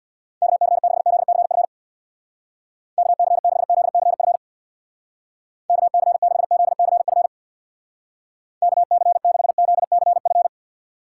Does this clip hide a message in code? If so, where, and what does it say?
Yes: Morse code from 0.4 to 10.5 s, heard as a keyed tone, about 65 wpm, 700 Hz. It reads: KC6ZCF KC6ZCF KC6ZCF KC6ZCF